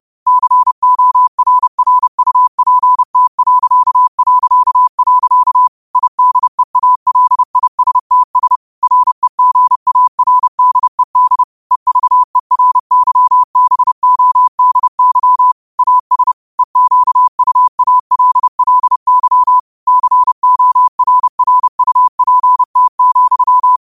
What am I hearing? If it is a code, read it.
Morse code, heard as a keyed tone, about 30 wpm, 1000 Hz. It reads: CORRUPT... IDEALISTS REGARDED EVERYBODY AS EQUALLY CORRUPT,